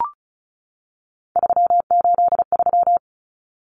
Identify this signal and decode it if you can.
Morse code, heard as a keyed tone, about 34 wpm, 700 Hz. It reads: 383
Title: Numbers - 3-Digit 35wpm